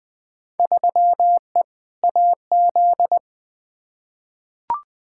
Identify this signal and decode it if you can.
Morse code, heard as a keyed tone, about 20 wpm, 700 Hz. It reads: 3E AZ
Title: Field Day Exchanges Encore - 20wpm